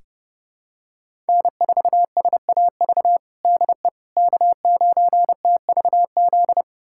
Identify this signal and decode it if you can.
Morse code, heard as a keyed tone, about 32 wpm, 700 Hz. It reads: N4SAV DE K9TVZ
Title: USA Callsign DE Callsign - 5 Characters - 30wpm